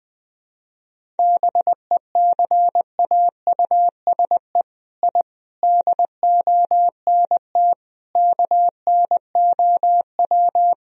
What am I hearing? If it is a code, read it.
Morse code, heard as a keyed tone, about 20 wpm, 700 Hz. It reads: BECAUSE I DONT KNOW
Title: Sentences from Top 1000 Words - Review Part 3 - 20wpm